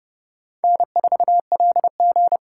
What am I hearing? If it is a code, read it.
Morse code, heard as a keyed tone, about 30 wpm, 700 Hz. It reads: N4LG
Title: USA Call Signs - 4 Characters - 3x Character Spacing - 30wpm